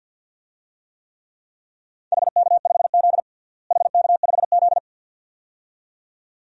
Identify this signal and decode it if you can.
Morse code, heard as a keyed tone, about 50 wpm, 700 Hz. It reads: HK5Z HK5Z
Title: DXCC-100 Call Signs - 4 Characters - Repeated 2x - 50wpm